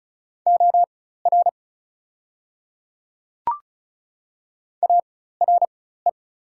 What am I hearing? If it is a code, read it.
Morse code, heard as a keyed tone, about 34 wpm, spaced as for 16 wpm, 690 Hz. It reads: OR ARE
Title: 4x Character Spacing - Top 100 Words - 35wpm